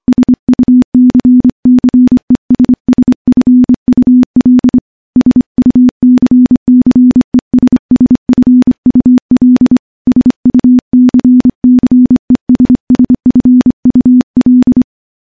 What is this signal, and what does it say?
Morse code, heard as a keyed tone, about 25 wpm, 250 Hz. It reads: SUCCESSFUL SUCCESSFUL SUCCESSFUL
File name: morse-training-25WPM